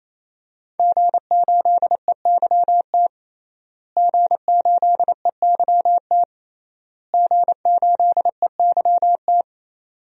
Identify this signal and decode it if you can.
Morse code, heard as a keyed tone, about 28 wpm, 700 Hz. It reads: G8EYT G8EYT G8EYT
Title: DXCC-100 Call Signs - 5 Characters - Repeated 3x - 28wpm